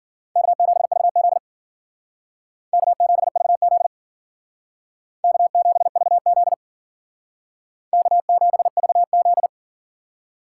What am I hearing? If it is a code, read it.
Morse code, heard as a keyed tone, about 49 wpm, 700 Hz. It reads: K7VZ K7VZ K7VZ K7VZ
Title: Speed Racing - USA 1x2 Call Signs 40wpm